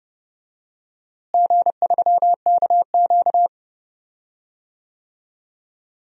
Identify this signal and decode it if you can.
Morse code, heard as a keyed tone, about 30 wpm, 700 Hz. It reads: G3KQ
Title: DXCC-100 Call Signs - 4 Characters - Repeated 2x - 30wpm